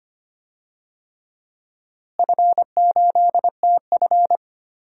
Morse code, heard as a keyed tone, about 25 wpm, 700 Hz. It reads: F8TF
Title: DXCC-100 Call Signs - 4 Characters - Repeated 3x - 25wpm